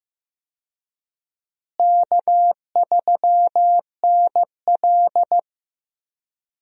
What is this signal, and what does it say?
Morse code, heard as a keyed tone, about 15 wpm, 700 Hz. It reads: K3NL
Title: USA Call Signs - 4 Characters - Repeated 3x - 15wpm